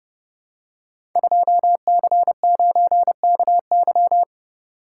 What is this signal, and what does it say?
Morse code, heard as a keyed tone, about 28 wpm, 700 Hz. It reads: 2C9KY
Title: Code Groups - 5 Letter:Numbers - 30wpm